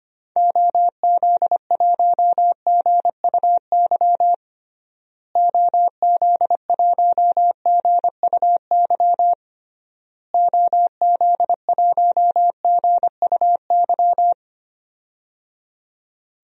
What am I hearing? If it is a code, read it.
Morse code, heard as a keyed tone, about 25 wpm, 700 Hz. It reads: OZ1GUY OZ1GUY OZ1GUY
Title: DXCC-100 Call Signs - 6 Characters - Repeated 3x - 25wpm